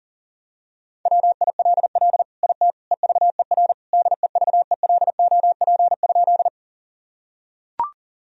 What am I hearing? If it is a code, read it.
Morse code, heard as a keyed tone, about 37 wpm, 700 Hz. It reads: WILL IT EVER DEVELOP?
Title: Sentences from Top 1000 Words - Review Part 2 - 40wpm